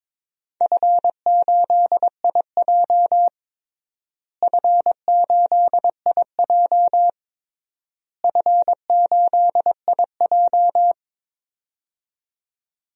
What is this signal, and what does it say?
Morse code, heard as a keyed tone, about 22 wpm, 700 Hz. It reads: F8IJ F8IJ F8IJ
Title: DXCC-100 Call Signs - 4 Characters - Repeated 3x - 22wpm